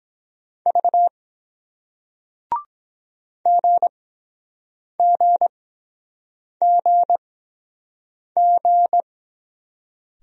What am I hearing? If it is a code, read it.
Morse code, heard as a keyed tone, about 23 wpm, spaced as for 7 wpm, 700 Hz. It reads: V GGGG